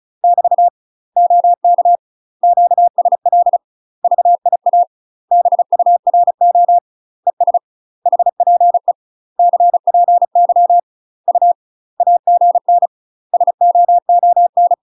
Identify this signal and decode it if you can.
Morse code, heard as a keyed tone, about 34 wpm, 700 Hz. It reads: <BT> OK QSL VIA BURO ES HPE CPY U AGN SOON
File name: UK MASTER QSOs TRAINING / 35WPM / UK MASTER QSOs 0028